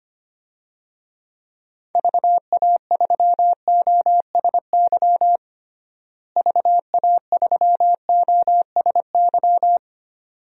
Morse code, heard as a keyed tone, about 25 wpm, 700 Hz. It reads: VA3OSY VA3OSY
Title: DXCC-100 Call Signs - 6 Characters - Repeated 2x - 25wpm